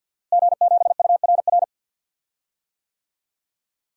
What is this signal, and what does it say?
Morse code, heard as a keyed tone, about 52 wpm, 700 Hz. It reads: G7URR
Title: DXCC-100 Call Signs - 5 Characters - Repeated 3x - 50wpm